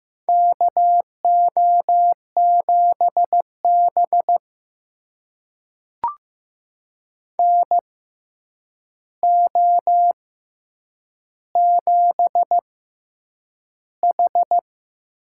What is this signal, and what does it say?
Morse code, heard as a keyed tone, about 15 wpm, spaced as for 8 wpm, 700 Hz. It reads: KO7B NO7H
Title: USA Call Signs - 6x Character Spacing - 15wpm